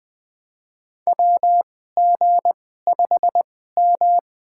Morse code, heard as a keyed tone, about 20 wpm, spaced as for 14 wpm, 700 Hz. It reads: WG5M
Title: USA Call Signs - 2x Character Spacing - 20wpm